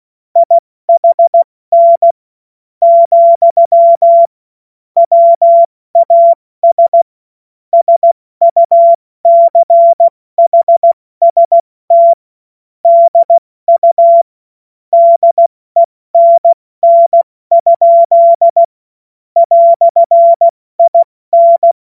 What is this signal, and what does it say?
Morse code, heard as a keyed tone, about 16 wpm, spaced as for 14 wpm, 680 Hz. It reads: IHN , WAS SUCHST DU DENN? "IN
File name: Morsekurs / Witze / Witz 20. Januar 2026 70 F80